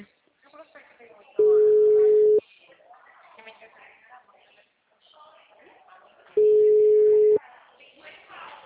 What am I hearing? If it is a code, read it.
Ringback tone (Europe, 425 Hz), with speech.
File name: out- -2003- - - .2472